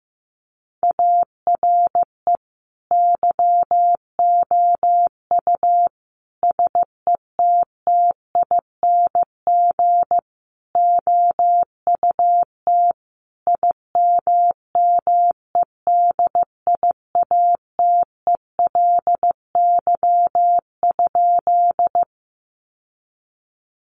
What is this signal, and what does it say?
Morse code, heard as a keyed tone, about 15 wpm, 700 Hz. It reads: ARE YOU SETTING OUT IMMEDIATELY?
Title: Sentences from Top 1300 Words 15wpm